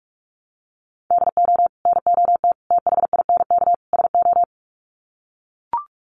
Morse code, heard as a keyed tone, about 44 wpm, 700 Hz. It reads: DO NOT THINK SO
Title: Sets of 4 Words 45wpm